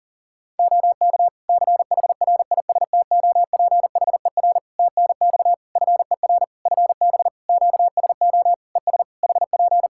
Morse code, heard as a keyed tone, about 40 wpm, 700 Hz. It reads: OK CHRISTOPHER TNX FER FB QSO ES HP
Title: Practice QSO - Exchange 4 - Ending - 40wpm